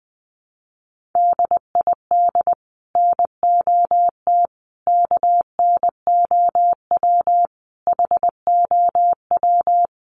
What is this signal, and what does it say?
Morse code, heard as a keyed tone, about 20 wpm, 700 Hz. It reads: DID NOT KNOW HOW
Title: Sets of 5 Words 20wpm